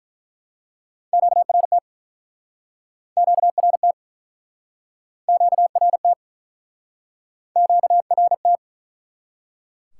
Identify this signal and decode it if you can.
Morse code, heard as a keyed tone, about 48 wpm, 700 Hz. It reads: QRT QRT QRT QRT